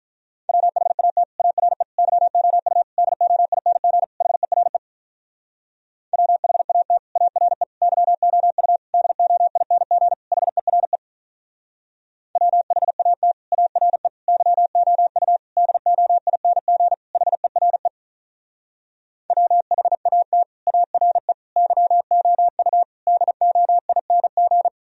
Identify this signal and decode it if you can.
Morse code, heard as a keyed tone, about 46 wpm, 700 Hz. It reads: WHAT ARE YOU DOING HERE WHAT ARE YOU DOING HERE WHAT ARE YOU DOING HERE WHAT ARE YOU DOING